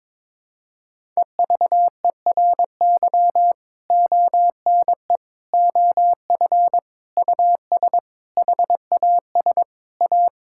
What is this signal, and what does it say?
Morse code, heard as a keyed tone, about 22 wpm, 700 Hz. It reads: EVERY ONE OF US HAS A
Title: Sentences from Top 1000 Words - Review Part 6 - 22wpm